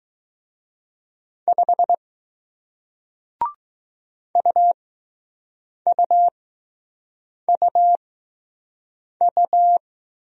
Morse code, heard as a keyed tone, about 20 wpm, spaced as for 6 wpm, 700 Hz. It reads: 5 UUUU